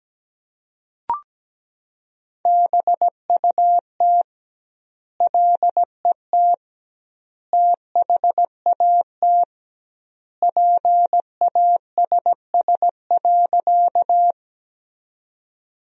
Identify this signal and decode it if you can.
Morse code, heard as a keyed tone, about 17 wpm, 700 Hz. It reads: BUT LET THAT PASS.
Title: 2x Word-Spacing - Sentences from Top 400 Words - 17wpm